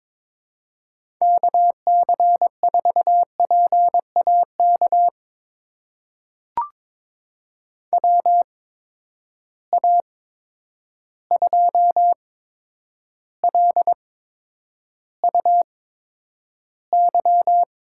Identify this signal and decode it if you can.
Morse code, heard as a keyed tone, about 22 wpm, spaced as for 9 wpm, 700 Hz. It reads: KC4PAK WA2LUY
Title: USA Call Signs - 6 Characters - 8x Character Spacing - 22wpm